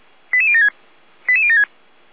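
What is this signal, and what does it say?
Five-tone selective call (ZVEI), heard as quick runs of five tones, tones of about 70 ms. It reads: 88076 88076
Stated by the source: ZVEI five-tone call 88082 received by SDR.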